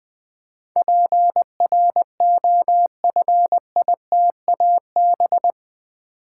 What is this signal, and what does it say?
Morse code, heard as a keyed tone, about 20 wpm, 700 Hz. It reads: PROFITAB
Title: Build A Word - 4 Letters At A Time - 10-Letter Words - 20wpm